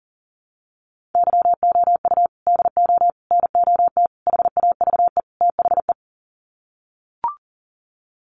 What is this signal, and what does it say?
Morse code, heard as a keyed tone, about 40 wpm, 700 Hz. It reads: YOU DO NOT HAVE THE
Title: Sets of 5 Words 40wpm